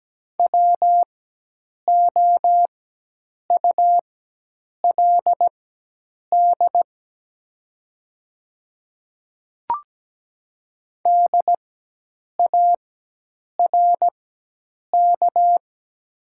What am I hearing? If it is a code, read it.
Morse code, heard as a keyed tone, about 17 wpm, spaced as for 8 wpm, 700 Hz. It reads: WOULD DARK
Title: 4x Character Spacing - Top 500 Words - 17wpm